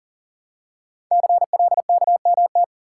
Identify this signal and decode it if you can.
Morse code, heard as a keyed tone, about 38 wpm, 700 Hz. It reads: CLKMT
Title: Code Groups - 5 Letters - 40wpm